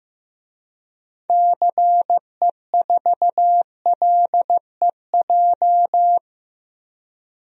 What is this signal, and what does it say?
Morse code, heard as a keyed tone, about 15 wpm, 700 Hz. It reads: CE4LEJ
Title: DXCC-100 Call Signs - 6 Characters - 15wpm